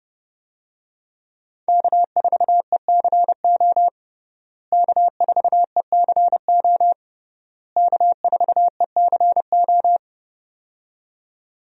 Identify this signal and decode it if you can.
Morse code, heard as a keyed tone, about 30 wpm, 700 Hz. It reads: K4ECO K4ECO K4ECO
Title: USA Call Signs - 5 Characters - Repeated 3x - 30wpm